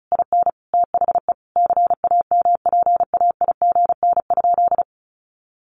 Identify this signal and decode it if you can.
Morse code, heard as a keyed tone, about 34 wpm, 700 Hz. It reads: IN THE CAMPAIGN?
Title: Sentences from Top 1200 Words 35wpm